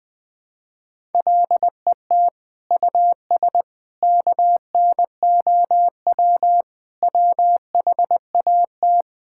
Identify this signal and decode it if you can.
Morse code, heard as a keyed tone, about 20 wpm, 700 Hz. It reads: LET US KNOW WHAT
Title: Sentences with 8-Word Phrases - Spoken Beginning - 20wpm